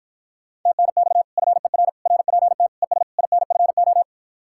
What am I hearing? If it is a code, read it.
Morse code, heard as a keyed tone, about 52 wpm, 700 Hz. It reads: TNX FER RPT ES INFO